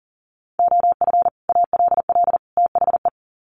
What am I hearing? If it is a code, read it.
Morse code, heard as a keyed tone, about 40 wpm, 700 Hz. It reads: OF ALL THE
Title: Sets of 3 Words 40wpm